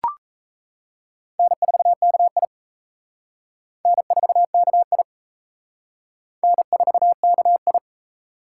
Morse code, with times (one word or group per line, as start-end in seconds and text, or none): 1.4-2.5 N4KI
3.9-5.0 N4KI
6.4-7.8 N4KI